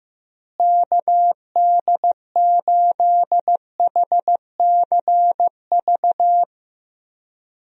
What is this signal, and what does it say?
Morse code, heard as a keyed tone, about 16 wpm, 700 Hz. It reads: KD8HCV
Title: USA Call Signs - 6 Characters - Repeated 2x - 15wpm